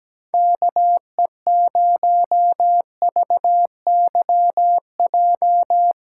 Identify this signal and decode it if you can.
Morse code, heard as a keyed tone, about 17 wpm, 700 Hz. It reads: KE0VYJ
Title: USA Call Signs - 6 Characters - 8x Character Spacing - 17wpm